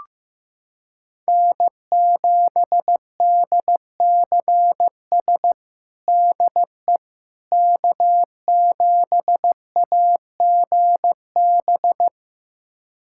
Morse code, heard as a keyed tone, about 15 wpm, 700 Hz. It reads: N7DCS DE K7AGB
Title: USA Callsign DE Callsign 15wpm